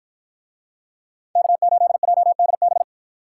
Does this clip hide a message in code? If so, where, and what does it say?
Yes: Morse code from 1.4 to 2.8 s, heard as a keyed tone, about 53 wpm, 700 Hz. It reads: K8JDB